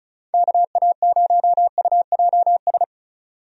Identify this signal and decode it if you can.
Morse code, heard as a keyed tone, about 34 wpm, 700 Hz. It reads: KA0UJS
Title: USA Call Signs - 6 Characters - 35wpm